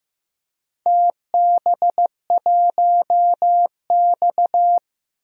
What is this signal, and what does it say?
Morse code, heard as a keyed tone, about 15 wpm, 700 Hz. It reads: TB1X
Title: DXCC-100 Call Signs - 4 Characters - Repeated 2x - 15wpm